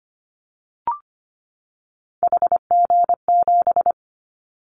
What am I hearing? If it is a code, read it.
Morse code, heard as a keyed tone, about 25 wpm, 700 Hz. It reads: HG7
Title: Callsign Prefixes for Top 50 DXCC - 25wpm